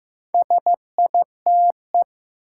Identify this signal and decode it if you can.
Morse code, heard as a keyed tone, about 15 wpm, 700 Hz. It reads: SITE
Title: Sentences from Top 1000 Words - Review Part 1 - New Word Drill - 15wpm